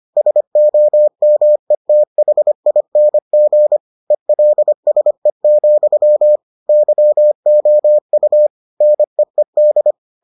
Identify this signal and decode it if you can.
Morse code, heard as a keyed tone, about 25 wpm, 600 Hz. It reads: SOMETHING ELSE, YOU NEED